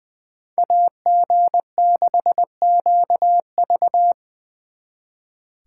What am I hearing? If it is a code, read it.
Morse code, heard as a keyed tone, about 25 wpm, 700 Hz. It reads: AG6QV
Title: Speed Racing - USA 2x2 Call Signs 20wpm